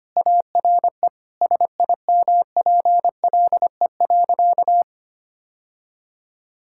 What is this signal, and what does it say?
Morse code, heard as a keyed tone, about 25 wpm, 700 Hz. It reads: ARE SIMPLE.
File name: practice / 25wpm / Sentences with 3-Word Phrases - 25wpm